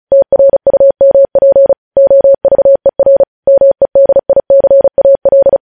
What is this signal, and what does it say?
Morse code, heard as a keyed tone, about 36 wpm, 550 Hz. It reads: TRUMP OVER MEDICAL